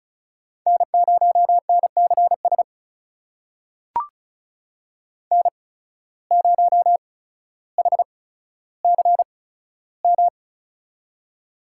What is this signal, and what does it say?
Morse code, heard as a keyed tone, about 34 wpm, spaced as for 15 wpm, 700 Hz. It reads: N0NCS N0HCM
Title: USA Call Signs - 5 Characters - 8x Character Spacing - 35wpm